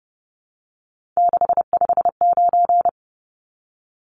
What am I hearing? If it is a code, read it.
Morse code, heard as a keyed tone, about 30 wpm, 700 Hz. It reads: 659
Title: Numbers - 3-Digit 30wpm